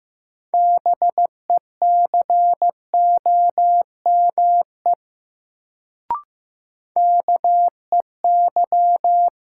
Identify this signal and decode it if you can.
Morse code, heard as a keyed tone, about 15 wpm, 700 Hz. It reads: BECOME KEY
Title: Sentences from Top 600 Words - New Word Drill - 15wpm